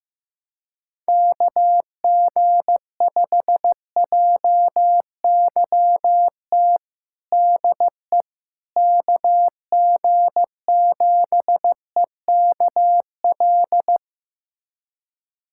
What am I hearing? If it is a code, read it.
Morse code, heard as a keyed tone, about 15 wpm, 700 Hz. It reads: KG5JYT DE KG7EKL
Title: USA Callsign DE Callsign 15wpm